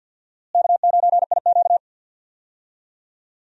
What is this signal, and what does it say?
Morse code, heard as a keyed tone, about 47 wpm, 700 Hz. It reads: K9IQ
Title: USA Call Signs - 4 Characters - Repeated 2x - 50wpm